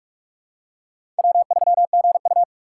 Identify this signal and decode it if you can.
Morse code, heard as a keyed tone, about 46 wpm, 700 Hz. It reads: W3GU
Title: USA Call Signs - 4 Characters - 4x Character Spacing - 45wpm